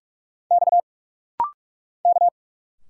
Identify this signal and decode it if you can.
Morse code, heard as a keyed tone, about 46 wpm, 700 Hz. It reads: X K
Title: Single Letters - Rapid-Fire 45wpm